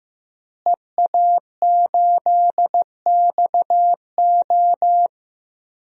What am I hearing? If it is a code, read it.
Morse code, heard as a keyed tone, about 16 wpm, 700 Hz. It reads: EA8XO
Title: DXCC-100 Call Signs - 5 Characters - Repeated 3x - 15wpm